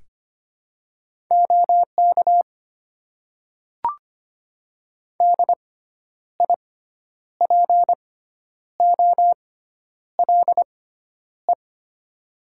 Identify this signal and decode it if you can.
Morse code, heard as a keyed tone, about 25 wpm, spaced as for 10 wpm, 690 Hz. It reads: OK DIPOLE
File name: Top 90 QSO Elements - 6x Character Spacing - 25wpm